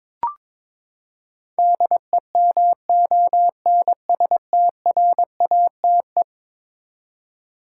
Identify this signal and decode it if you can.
Morse code, heard as a keyed tone, about 22 wpm, 700 Hz. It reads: DEMONSTRATE
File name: practice / 22wpm / Eleven Letter Words - 22wpm